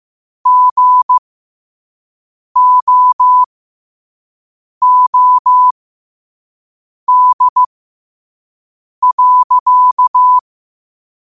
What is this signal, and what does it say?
Morse code, heard as a keyed tone, about 15 wpm, spaced as for 5 wpm, 1000 Hz. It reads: GOOD.
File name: 5wpm